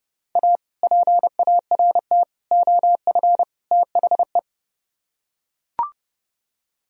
Morse code, heard as a keyed tone, about 31 wpm, 700 Hz. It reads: A PART OF THE